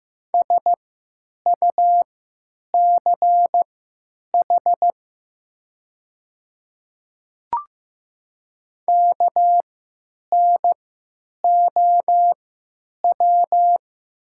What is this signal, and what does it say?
Morse code, heard as a keyed tone, about 15 wpm, spaced as for 9 wpm, 700 Hz. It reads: SUCH KNOW
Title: 3x Character Spacing - Top 200 Words - 15wpm